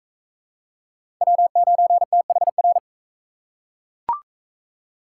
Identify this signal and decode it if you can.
Morse code, heard as a keyed tone, about 42 wpm, 700 Hz. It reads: W9THR